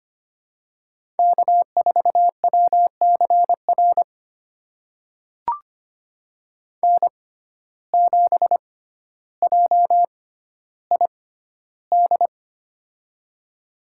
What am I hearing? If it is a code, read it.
Morse code, heard as a keyed tone, about 25 wpm, spaced as for 13 wpm, 700 Hz. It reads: K4WCR N7JID
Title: USA Call Signs - 6x Character Spacing - 25wpm